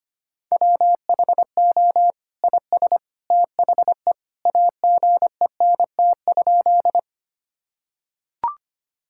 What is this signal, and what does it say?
Morse code, heard as a keyed tone, about 25 wpm, 700 Hz. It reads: WHO IS THE AGENT?
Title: Sentences from Top 1600 Words 25wpm